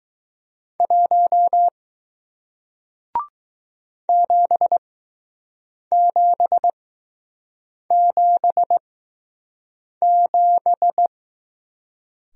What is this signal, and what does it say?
Morse code, heard as a keyed tone, about 20 wpm, spaced as for 6 wpm, 700 Hz. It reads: 1 7777